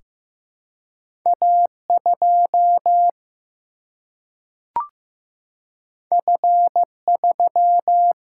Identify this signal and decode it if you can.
Morse code, heard as a keyed tone, about 15 wpm, 700 Hz. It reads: A2 F3
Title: Single-Letter Single-Number 15wpm